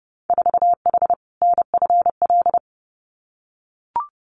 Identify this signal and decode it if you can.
Morse code, heard as a keyed tone, about 30 wpm, 700 Hz. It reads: 4H NFL
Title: Winter Field Day Exchanges Encore 30wpm